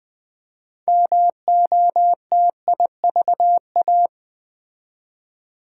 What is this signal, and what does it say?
Morse code, heard as a keyed tone, about 20 wpm, 700 Hz. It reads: MOTIVA
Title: Build A Word - 2 Letters At A Time - 8-Letter Words - 20wpm